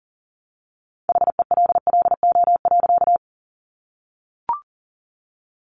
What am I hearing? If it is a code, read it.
Morse code, heard as a keyed tone, about 40 wpm, 700 Hz. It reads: HELLO.
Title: Sentences from Top 1100 Words 40wpm